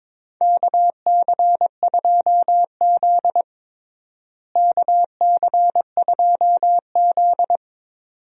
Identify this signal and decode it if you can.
Morse code, heard as a keyed tone, about 22 wpm, 700 Hz. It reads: KC2Z KC2Z
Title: USA Call Signs - 4 Characters - Repeated 2x - 22wpm